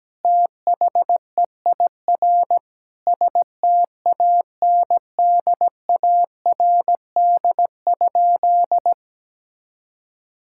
Morse code, heard as a keyed tone, about 17 wpm, 700 Hz. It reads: THEIR STANDARD?
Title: Sentences from Top 1000 Words - Review Part 6 - 17wpm